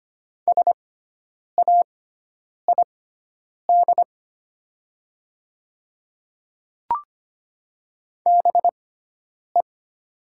Morse code, heard as a keyed tone, about 25 wpm, spaced as for 9 wpm, 700 Hz. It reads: SAID BE
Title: 6x Character Spacing - Top 100 Words - 25wpm